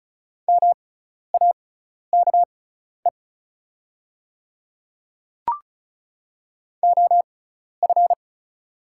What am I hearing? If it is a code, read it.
Morse code, heard as a keyed tone, about 36 wpm, spaced as for 12 wpm, 700 Hz. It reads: MAKE OF
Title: 6x Character Spacing - Top 100 Words - 35wpm